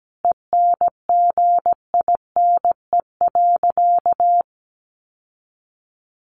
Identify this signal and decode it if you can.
Morse code, heard as a keyed tone, about 17 wpm, 700 Hz. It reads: ENGINE.
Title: Sentences from Top 1500 Words 17wpm